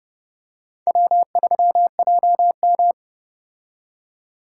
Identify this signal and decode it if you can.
Morse code, heard as a keyed tone, about 30 wpm, 700 Hz. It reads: W3JM